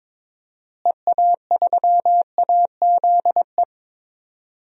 Morse code, heard as a keyed tone, about 21 wpm, 700 Hz. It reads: EA3AZE
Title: DXCC-100 Call Signs - 6 Characters - Repeated 3x - 22wpm